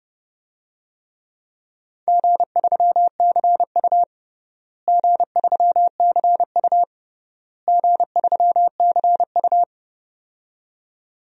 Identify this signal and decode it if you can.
Morse code, heard as a keyed tone, about 29 wpm, 700 Hz. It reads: G3CU G3CU G3CU
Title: DXCC-100 Call Signs - 4 Characters - Repeated 3x - 30wpm